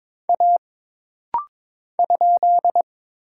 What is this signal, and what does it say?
Morse code, heard as a keyed tone, about 22 wpm, 700 Hz. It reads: A ?
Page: This rapid-fire timing is geared to push your ICR (Instant Character Recognition) skills.